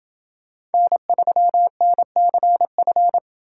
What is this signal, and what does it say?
Morse code, heard as a keyed tone, about 29 wpm, 700 Hz. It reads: N3NCF